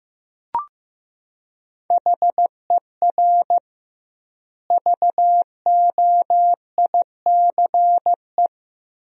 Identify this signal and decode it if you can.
Morse code, heard as a keyed tone, about 15 wpm, 700 Hz. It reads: HER VOICE
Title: 2x Word-Spacing - Sentences from Top 500 Words - 15wpm